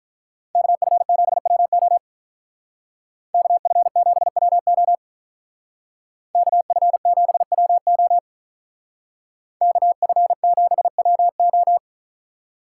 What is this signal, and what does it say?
Morse code, heard as a keyed tone, about 45 wpm, 700 Hz. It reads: KF7WO KF7WO KF7WO KF7WO